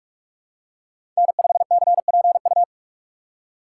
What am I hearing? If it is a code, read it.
Morse code, heard as a keyed tone, about 43 wpm, 700 Hz. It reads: N5CPU
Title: USA Call Signs - 2x Character Spacing - 45wpm